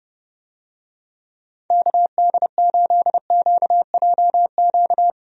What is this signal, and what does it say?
Morse code, heard as a keyed tone, about 32 wpm, 700 Hz. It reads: KD8QJQ
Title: USA Call Signs - 6 Characters - Repeated 2x - 30wpm